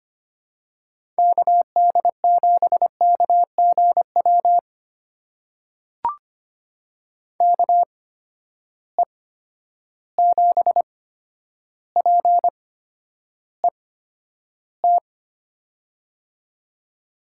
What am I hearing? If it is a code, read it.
Morse code, heard as a keyed tone, about 25 wpm, spaced as for 11 wpm, 700 Hz. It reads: KD7KGW KE7PET
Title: USA Call Signs - 6 Characters - 8x Character Spacing - 25wpm